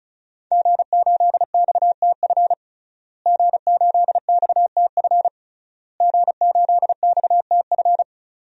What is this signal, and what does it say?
Morse code, heard as a keyed tone, about 36 wpm, 700 Hz. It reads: G8XTF G8XTF G8XTF
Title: DXCC-100 Call Signs - 5 Characters - Repeated 3x - 35wpm